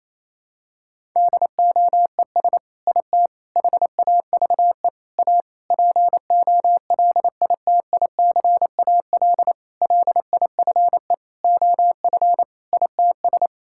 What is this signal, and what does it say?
Morse code, heard as a keyed tone, about 28 wpm, 700 Hz. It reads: DOES IT HAVE A POLITICAL LIFE OF ITS
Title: Sentences from Top 1000 Words - Review Part 5 - 28wpm